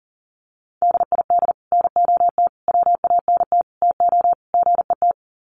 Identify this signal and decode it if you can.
Morse code, heard as a keyed tone, about 40 wpm, 700 Hz. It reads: DID NOT WANT TO GET
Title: Sets of 5 Words Encore2 40wpm